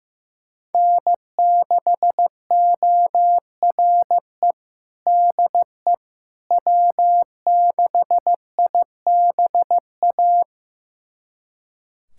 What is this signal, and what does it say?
Morse code, heard as a keyed tone, about 15 wpm, 700 Hz. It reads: N6ORE DE W6IBA
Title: USA Callsign DE Callsign - 5 Characters - 15wpm